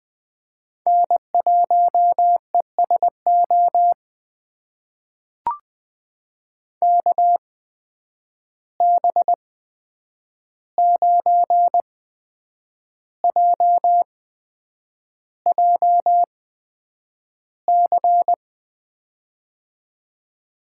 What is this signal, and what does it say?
Morse code, heard as a keyed tone, about 20 wpm, spaced as for 8 wpm, 700 Hz. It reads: N1ESO KB9JJC
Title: USA Call Signs - 8x Character Spacing - 20wpm